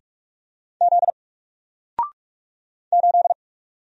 Morse code, heard as a keyed tone, about 44 wpm, 700 Hz. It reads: Z 8
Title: Single Letter-Number Alt 45wpm